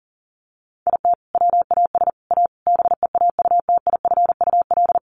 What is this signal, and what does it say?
Morse code, heard as a keyed tone, about 40 wpm, 700 Hz. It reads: IT WAS A BEAUTIFUL
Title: Sentences from Top 1200 Words 40wpm